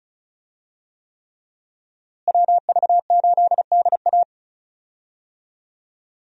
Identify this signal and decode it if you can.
Morse code, heard as a keyed tone, about 34 wpm, 700 Hz. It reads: WV8DA
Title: USA Call Signs - 5 Characters - Repeated 3x - 35wpm